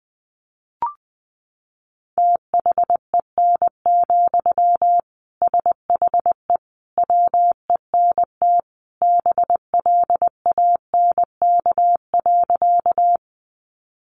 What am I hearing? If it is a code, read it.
Morse code, heard as a keyed tone, about 20 wpm, 700 Hz. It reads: THEN, SHE WENT BLANK.
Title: Sentences from Top 1500 Words 20wpm